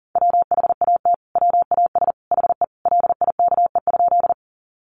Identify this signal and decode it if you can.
Morse code, heard as a keyed tone, about 37 wpm, 700 Hz. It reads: WHAT WAS HE LIKE?
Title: Sentences from Top 100 Words 40wpm